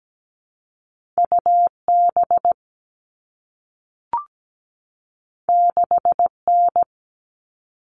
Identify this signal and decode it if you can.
Morse code, heard as a keyed tone, about 17 wpm, 700 Hz. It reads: UB 6N
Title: All Cut Numbers - 2-Digit 17wpm